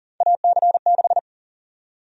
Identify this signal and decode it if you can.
Morse code, heard as a keyed tone, about 40 wpm, 700 Hz. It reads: AC6
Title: USA Call Sign Prefixes - 40wpm